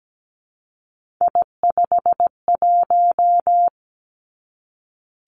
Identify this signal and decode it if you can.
Morse code, heard as a keyed tone, about 17 wpm, 700 Hz. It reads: I51
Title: Three Challenging Letter-Number Combinations 17wpm